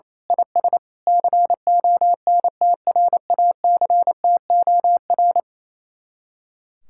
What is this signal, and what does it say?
Morse code, heard as a keyed tone, about 27 wpm, 700 Hz. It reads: IS CONTRACTOR
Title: Speed-Racing Common QSO Phrases - 28wpm